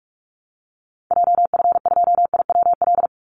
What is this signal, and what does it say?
Morse code, heard as a keyed tone, about 44 wpm, 700 Hz. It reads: JF2IWL
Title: Call Signs Repeated 3x - 45wpm